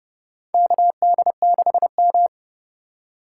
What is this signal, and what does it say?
Morse code, heard as a keyed tone, about 30 wpm, 690 Hz. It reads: KD6M
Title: USA Call Signs - 4 Characters - 2x Character Spacing - 30wpm